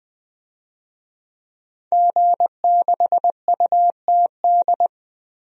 Morse code, heard as a keyed tone, about 20 wpm, 700 Hz. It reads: G6UTD
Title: DXCC-100 Call Signs - 5 Characters - 20wpm